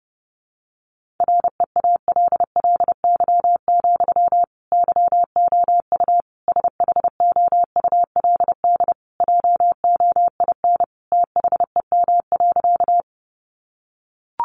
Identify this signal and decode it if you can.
Morse code, heard as a keyed tone, about 30 wpm, 700 Hz. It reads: REALLY, YOU SHOULD JOIN THEM.
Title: Sentences from Top 1400 Words 30wpm